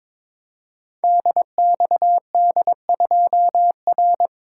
Morse code, heard as a keyed tone, about 22 wpm, 700 Hz. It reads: DXD2R
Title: Code Groups - 5 Letter:Numbers - 22wpm